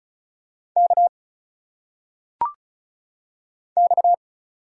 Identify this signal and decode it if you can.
Morse code, heard as a keyed tone, about 36 wpm, 700 Hz. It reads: K X
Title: Single Letters - Phonetic Pronunciation 35wpm